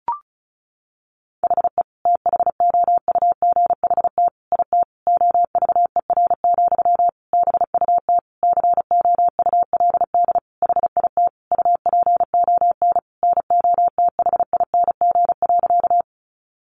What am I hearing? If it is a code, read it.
Morse code, heard as a keyed tone, about 34 wpm, 700 Hz. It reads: HE THOUGHT IT OVER, BUT COULD HIT UPON NOTHING.
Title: Sentences from Top 1200 Words 35wpm